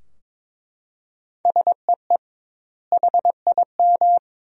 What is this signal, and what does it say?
Morse code, heard as a keyed tone, about 22 wpm, 700 Hz. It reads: SEE HIM
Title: 2x Word-Spacing - Sentences from Top 300 Words - 22wpm